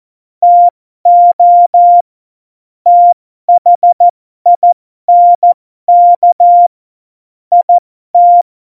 Morse code, heard as a keyed tone, about 14 wpm, spaced as for 12 wpm, 700 Hz. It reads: TO THINK IT
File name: Morsekurs / Witze